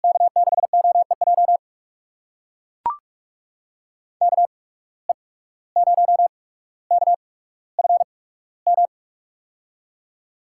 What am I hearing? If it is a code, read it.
Morse code, heard as a keyed tone, about 44 wpm, spaced as for 18 wpm, 700 Hz. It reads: K6OEJ KE0KFM
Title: USA Call Signs - 6 Characters - 8x Character Spacing - 45wpm